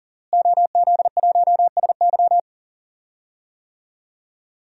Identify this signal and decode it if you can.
Morse code, heard as a keyed tone, about 40 wpm, 700 Hz. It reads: OZ1SY
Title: DXCC-100 Call Signs - 5 Characters - Repeated 2x - 40wpm